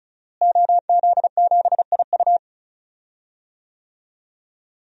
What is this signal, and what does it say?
Morse code, heard as a keyed tone, about 36 wpm, 700 Hz. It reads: OZ7IU
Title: DXCC-100 Call Signs - 5 Characters - Repeated 2x - 35wpm